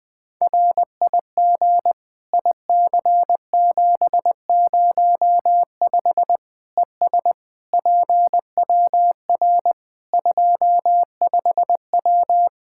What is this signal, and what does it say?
Morse code, heard as a keyed tone, about 20 wpm, 700 Hz. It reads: RIG IC705 ES PWR 25W
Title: Practice QSO - Exchange 2 - Info - 20wpm